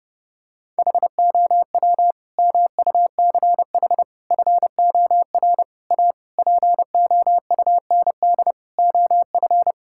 Morse code, heard as a keyed tone, about 31 wpm, 700 Hz. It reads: HOW MUCH FOR A POUND OF
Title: Sentences from Top 1000 Words - Review Part 5 - 30wpm